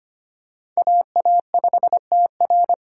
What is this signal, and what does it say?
Morse code, heard as a keyed tone, about 25 wpm, 700 Hz. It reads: AA5TR
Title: USA Call Signs - 5 Characters - 3x Character Spacing - 25wpm